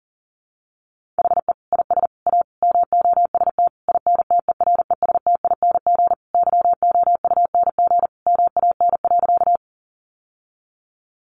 Morse code, heard as a keyed tone, about 40 wpm, 700 Hz. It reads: HE IS A MOST INTERESTING YOUNG MAN.
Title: Sentences from Top 1000 Words 40wpm